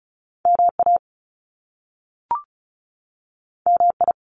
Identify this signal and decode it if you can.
Morse code, heard as a keyed tone, about 36 wpm, 700 Hz. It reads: MA MI
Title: US States 35wpm